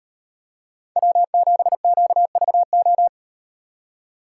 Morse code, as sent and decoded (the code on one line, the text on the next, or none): .-- --... --.- ...- ---
W7QVO